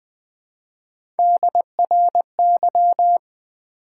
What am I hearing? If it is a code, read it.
Morse code, heard as a keyed tone, about 20 wpm, 700 Hz. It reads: DRY
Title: Build A Word - 1 Letter At A Time - 3-Letter Words - 20wpm